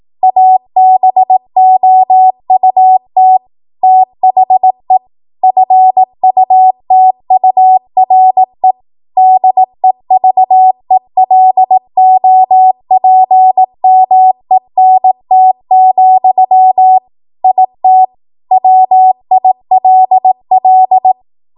Morse code, heard as a keyed tone, about 18 wpm, 750 Hz. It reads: ABOUT THE FUTURE DEVELOPMENT, IT WILL
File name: Morse / 18WPM